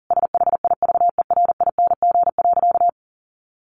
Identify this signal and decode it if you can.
Morse code, heard as a keyed tone, about 40 wpm, 700 Hz. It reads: SHIVERING.
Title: A Sound Of Thunder 40wpm